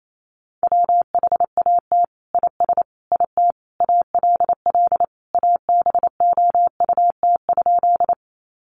Morse code, heard as a keyed tone, about 28 wpm, 700 Hz. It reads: WHAT IS IT ALL ABOUT?
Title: Sentences from Top 100 Words 28wpm